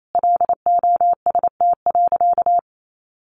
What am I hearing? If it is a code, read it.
Morse code, heard as a keyed tone, about 28 wpm, 700 Hz. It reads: LOST.
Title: Sentences from Top 1400 Words 28wpm